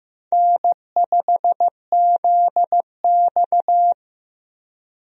Morse code, heard as a keyed tone, about 15 wpm, 700 Hz. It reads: N5ZX
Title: USA Call Signs - 4 Characters - Repeated 3x - 15wpm